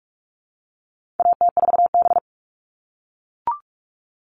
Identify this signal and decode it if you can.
Morse code, heard as a keyed tone, about 44 wpm, 700 Hz. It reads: AT4B
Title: All Cut Numbers - 4-Digit 45wpm